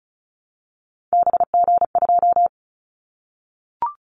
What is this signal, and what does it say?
Morse code, heard as a keyed tone, about 36 wpm, 700 Hz. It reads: BG2
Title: Callsign Prefixes for Top 100 DXCC - 35wpm